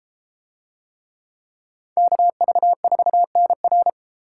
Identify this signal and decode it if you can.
Morse code, heard as a keyed tone, about 31 wpm, 700 Hz. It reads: KV4NR